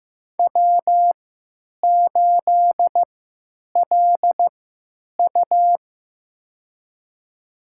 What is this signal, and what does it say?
Morse code, heard as a keyed tone, about 15 wpm, spaced as for 9 wpm, 700 Hz. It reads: W8LU
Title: USA Call Signs - 4 Characters - 3x Character Spacing - 15wpm